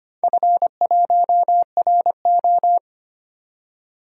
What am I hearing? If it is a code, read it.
Morse code, heard as a keyed tone, about 25 wpm, 700 Hz. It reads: F1RO
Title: DXCC-100 Call Signs - 4 Characters - 25wpm